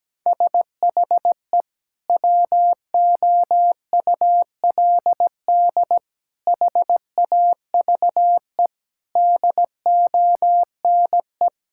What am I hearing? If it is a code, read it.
Morse code, heard as a keyed tone, about 17 wpm, 700 Hz. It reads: SHE WOULD HAVE DONE
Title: Sentences from Top 1600 Words 17wpm